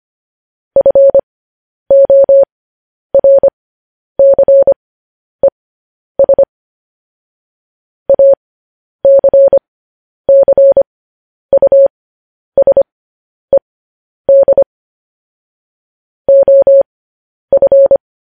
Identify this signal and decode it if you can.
Morse code, heard as a keyed tone, about 25 wpm, spaced as for 10 wpm, 550 Hz. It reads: FORCES ACCUSED OF